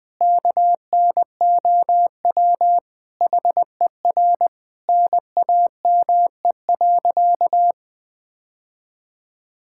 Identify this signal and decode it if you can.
Morse code, heard as a keyed tone, about 21 wpm, 700 Hz. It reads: KNOW HER NAME.
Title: Sentences with 5-Word Phrases - Spoken Beginning - 20wpm